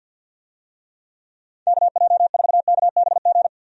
Morse code, heard as a keyed tone, about 52 wpm, 700 Hz. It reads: KJ4KBG
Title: USA Call Signs - 6 Characters - Repeated 2x - 50wpm